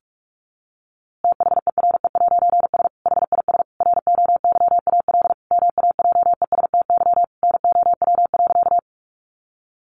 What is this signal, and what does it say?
Morse code, heard as a keyed tone, about 44 wpm, 700 Hz. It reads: THERE'S HIS ROYAL MAJESTY NOW.
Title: A Sound Of Thunder 45wpm